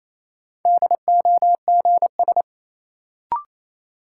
Morse code, heard as a keyed tone, about 27 wpm, 700 Hz. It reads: DOGS
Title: Sentences from Top 1600 Words - New Word Drill Alt - 28wpm